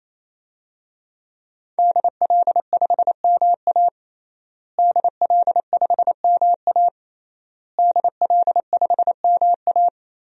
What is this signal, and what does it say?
Morse code, heard as a keyed tone, about 28 wpm, 700 Hz. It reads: DL5MA DL5MA DL5MA
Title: DXCC-100 Call Signs - 5 Characters - Repeated 3x - 28wpm